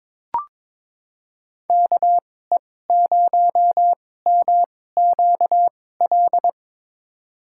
Morse code, heard as a keyed tone, about 22 wpm, spaced as for 16 wpm, 700 Hz. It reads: KE0MQL
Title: USA Call Signs - 6 Characters - 2x Character Spacing - 22wpm